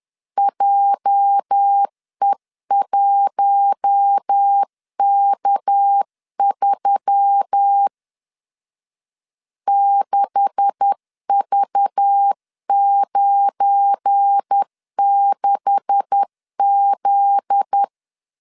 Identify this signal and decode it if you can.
Morse code, heard as a keyed tone, about 10 wpm, 800 Hz. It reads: JE1K3 6V96Z
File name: LEZIONI / 9 286 10wpm 2735